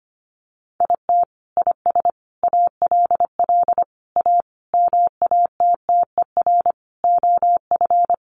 Morse code, heard as a keyed tone, about 25 wpm, 700 Hz. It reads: IT IS ALL A MATTER OF
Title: Sentences from Top 1300 Words 25wpm